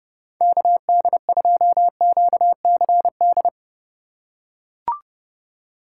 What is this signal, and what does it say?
Morse code, heard as a keyed tone, about 28 wpm, 700 Hz. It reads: KD2QCD